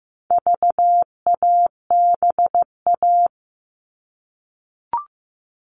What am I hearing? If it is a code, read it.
Morse code, heard as a keyed tone, about 15 wpm, 700 Hz. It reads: VABA
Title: All Cut Numbers - 4-Digit 15wpm